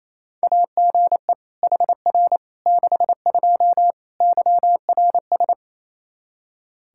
Morse code, heard as a keyed tone, about 28 wpm, 700 Hz. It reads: AGE HR 62 YRS
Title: Practice QSO - Exchange 3 - More Info - 28wpm